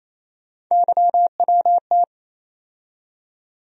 Morse code, heard as a keyed tone, about 28 wpm, 700 Hz. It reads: YWT
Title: USA Call Sign Postfixes - 3 Letters 28wpm